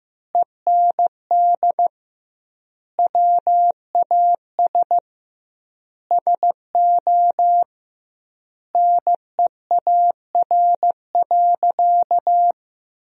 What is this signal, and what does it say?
Morse code, heard as a keyed tone, about 15 wpm, 700 Hz. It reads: END WAS SO NEAR.
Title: 2x Word-Spacing - Sentences from Top 300 Words - 15wpm